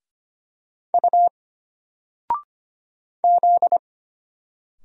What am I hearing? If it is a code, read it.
Morse code, heard as a keyed tone, about 25 wpm, 700 Hz. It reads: U Z
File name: Single-Letter-Number-25wpm